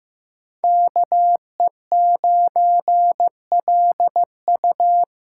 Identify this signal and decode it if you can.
Morse code, heard as a keyed tone, about 15 wpm, 700 Hz. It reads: KE9LU
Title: USA Callsign DE Callsign - 5 Characters - 15wpm